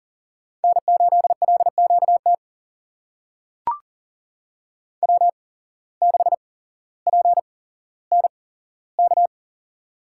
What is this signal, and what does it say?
Morse code, heard as a keyed tone, about 40 wpm, spaced as for 17 wpm, 700 Hz. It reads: N8LQT W6PNK
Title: USA Call Signs - 5 Characters - 8x Character Spacing - 40wpm